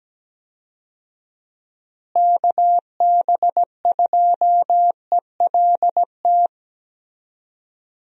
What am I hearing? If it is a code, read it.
Morse code, heard as a keyed tone, about 17 wpm, 700 Hz. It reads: KB2ELT
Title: USA Call Signs - 6 Characters - Repeated 2x - 17wpm